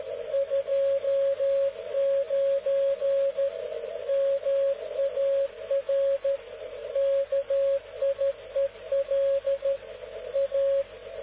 Morse code, heard as a keyed tone, about 13 wpm, 550 Hz. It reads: 29 MAR KIEL A